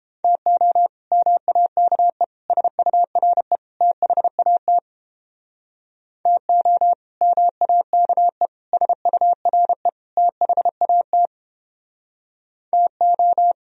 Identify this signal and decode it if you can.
Morse code, heard as a keyed tone, about 31 wpm, 700 Hz. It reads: TO MAKE SURE THAT TO MAKE SURE THAT TO